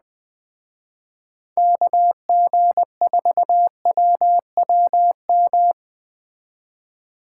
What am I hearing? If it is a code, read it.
Morse code, heard as a keyed tone, about 20 wpm, 700 Hz. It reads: KG4WWM